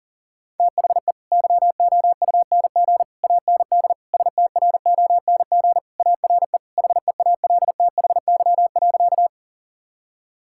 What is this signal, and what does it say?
Morse code, heard as a keyed tone, about 37 wpm, 700 Hz. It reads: THE YOUNG AND STRONG ARE HEALTHY.
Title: Sentences from Top 1000 Words - Review Part 5 - 40wpm